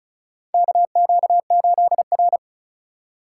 Morse code, heard as a keyed tone, about 34 wpm, 700 Hz. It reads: KQ8R
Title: USA Call Signs - 4 Characters - 2x Character Spacing - 35wpm